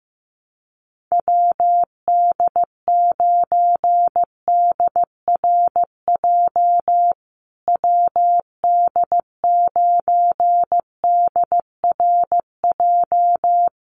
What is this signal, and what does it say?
Morse code, heard as a keyed tone, about 15 wpm, 700 Hz. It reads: WD9DRJ WD9DRJ
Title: USA Call Signs Repeated 3x 15wpm